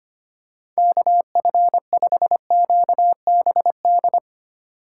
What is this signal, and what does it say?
Morse code, heard as a keyed tone, about 25 wpm, 700 Hz. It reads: KF5QBD
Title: USA Call Signs - 4x Character Spacing - 25wpm